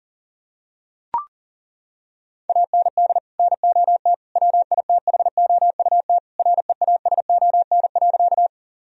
Morse code, heard as a keyed tone, about 37 wpm, 700 Hz. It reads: AND NOT WITHOUT REASON.
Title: Sentences from Top 1000 Words - Review Part 3 - 40wpm